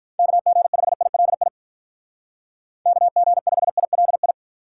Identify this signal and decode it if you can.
Morse code, heard as a keyed tone, about 45 wpm, 700 Hz. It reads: KG5ILI KG5ILI